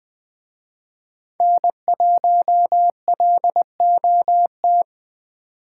Morse code, heard as a keyed tone, about 20 wpm, 700 Hz. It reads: N1LOT
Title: Speed Racing - USA 1x3 Call Signs 17wpm